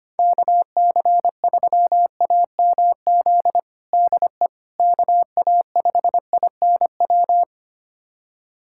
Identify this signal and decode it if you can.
Morse code, heard as a keyed tone, about 25 wpm, 700 Hz. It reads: KC3AMZ DE KA5INW
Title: USA Callsign DE Callsign - 6 Characters - 25wpm